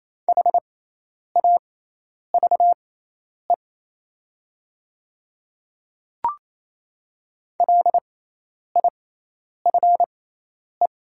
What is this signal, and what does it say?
Morse code, heard as a keyed tone, about 28 wpm, spaced as for 10 wpm, 700 Hz. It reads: HAVE LIFE
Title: 6x Character Spacing - Top 500 Words - 28wpm